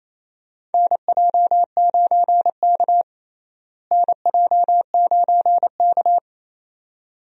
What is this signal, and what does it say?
Morse code, heard as a keyed tone, about 27 wpm, 700 Hz. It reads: NJ9K NJ9K
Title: USA Call Signs - 4 Characters - Repeated 2x - 28wpm